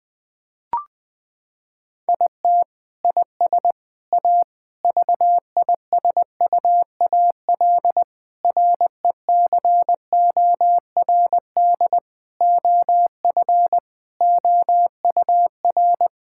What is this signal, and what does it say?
Morse code, heard as a keyed tone, about 20 wpm, 700 Hz. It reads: IT IS A VISUAL RECORD OF OUR
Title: Sentences from Top 1000 Words - Review Part 6 - 20wpm